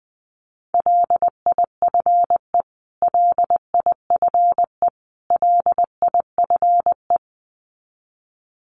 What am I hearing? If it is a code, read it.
Morse code, heard as a keyed tone, about 20 wpm, 700 Hz. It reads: LIFE LIFE LIFE
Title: Top 500 Words Repeated 20wpm